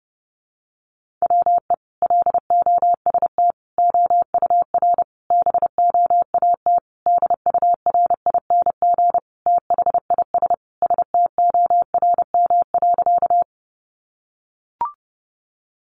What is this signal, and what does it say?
Morse code, heard as a keyed tone, about 30 wpm, 700 Hz. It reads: WE LOST OUR BOAT DURING THIS STORM.
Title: Sentences from Top 1500 Words 30wpm